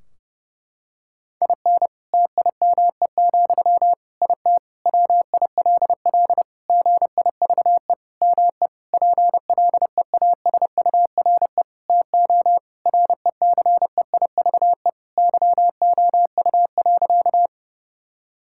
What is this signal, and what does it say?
Morse code, heard as a keyed tone, about 30 wpm, 700 Hz. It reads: IN TIME, IT WILL GIVE ME PLEASURE TO RECEIVE YOU.
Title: Sentences from Top 1600 Words 30wpm